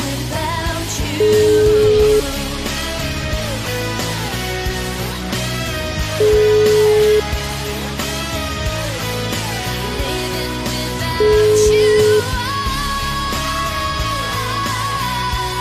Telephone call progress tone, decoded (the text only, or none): ringback tone